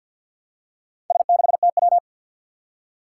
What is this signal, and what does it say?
Morse code, heard as a keyed tone, about 53 wpm, 700 Hz. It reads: S6TW
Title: Four Challenging Letter-Number Combinations 50wpm